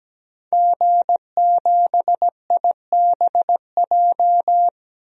Morse code, heard as a keyed tone, about 17 wpm, 700 Hz. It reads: G7IBJ
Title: DXCC-100 Call Signs - 5 Characters - Repeated 3x - 17wpm